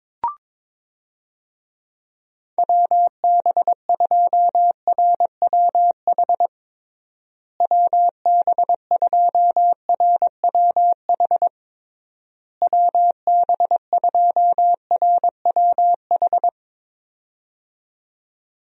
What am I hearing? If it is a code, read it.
Morse code, heard as a keyed tone, about 22 wpm, 700 Hz. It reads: WB2RWH WB2RWH WB2RWH
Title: USA Call Signs - 6 Characters - Repeated 3x - 22wpm